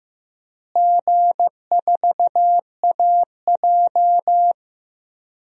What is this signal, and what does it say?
Morse code, heard as a keyed tone, about 15 wpm, 700 Hz. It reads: G4AJ
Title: DXCC-100 Call Signs - 4 Characters - Repeated 2x - 15wpm